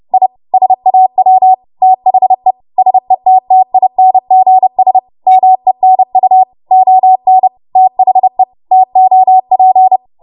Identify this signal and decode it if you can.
Morse code, heard as a keyed tone, about 30 wpm, 750 Hz. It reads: I SAW THE SETTINGS MENU ON THE TOP
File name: Morse / 30WPM